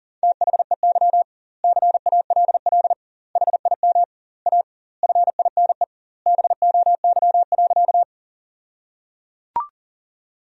Morse code, heard as a keyed tone, about 37 wpm, 700 Hz. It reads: THEY CALL HIM A FINE BOY.
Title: 2x Word-Spacing - Sentences from Top 500 Words - 40wpm